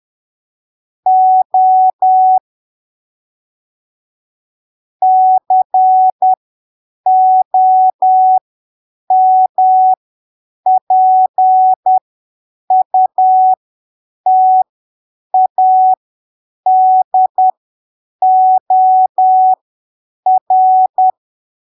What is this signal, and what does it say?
Morse code, heard as a keyed tone, about 10 wpm, spaced as for 7 wpm, 750 Hz. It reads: O COMPUTADOR
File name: arquivos / texto / texto 25 10ppm